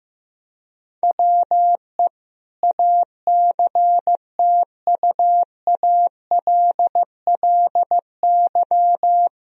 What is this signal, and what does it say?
Morse code, heard as a keyed tone, about 15 wpm, 700 Hz. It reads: WE ACTUALLY
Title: Sentences from Top 1000 Words - Review Part 4 - 15wpm